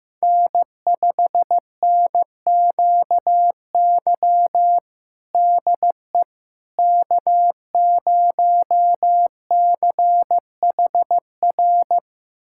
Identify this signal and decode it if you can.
Morse code, heard as a keyed tone, about 15 wpm, 700 Hz. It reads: N5NQY DE K0CHR
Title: USA Callsign DE Callsign - 5 Characters - 15wpm